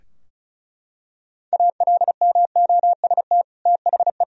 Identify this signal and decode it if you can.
Morse code, heard as a keyed tone, about 36 wpm, 700 Hz. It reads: ALMOST THE
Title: Sentences from Top 1000 Words - Review Part 4 - 35wpm